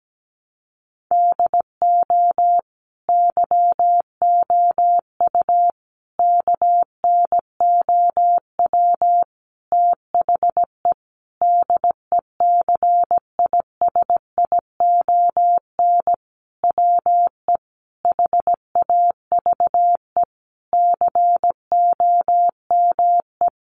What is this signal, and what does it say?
Morse code, heard as a keyed tone, about 17 wpm, 700 Hz. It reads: DO YOU KNOW THE DECISION WE HAVE COME
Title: Sentences from Top 1000 Words 17wpm